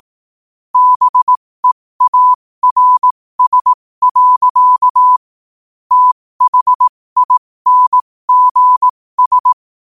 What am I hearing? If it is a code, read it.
Morse code, heard as a keyed tone, about 18 wpm, spaced as for 15 wpm, 1000 Hz. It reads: BEARS. THINGS
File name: qotdfiles / 15wpm